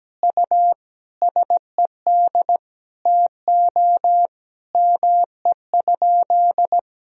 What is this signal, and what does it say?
Morse code, heard as a keyed tone, about 17 wpm, 700 Hz. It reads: U SED TO ME?
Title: Sentences from Top 200 Words and Abbreviations 17wpm